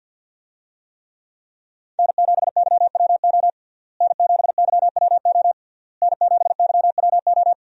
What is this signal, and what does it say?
Morse code, heard as a keyed tone, about 52 wpm, 700 Hz. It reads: N7YWO N7YWO N7YWO
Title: USA Call Signs - 5 Characters - Repeated 3x - 50wpm